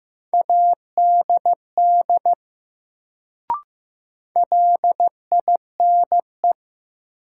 Morse code, heard as a keyed tone, about 15 wpm, 700 Hz. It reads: ADD LINE
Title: Sentences from Top 1000 Words - Review Part 1 - New Word Drill - 15wpm